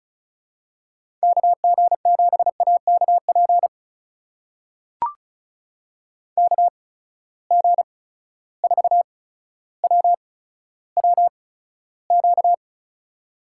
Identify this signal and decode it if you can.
Morse code, heard as a keyed tone, about 36 wpm, spaced as for 15 wpm, 700 Hz. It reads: KG7AKP KG4WWQ
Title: USA Call Signs - 6 Characters - 8x Character Spacing - 35wpm